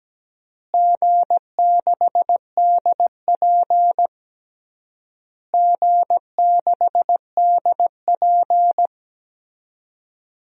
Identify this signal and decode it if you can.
Morse code, heard as a keyed tone, about 17 wpm, 700 Hz. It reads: G6DP G6DP
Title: DXCC-100 Call Signs - 4 Characters - Repeated 3x - 17wpm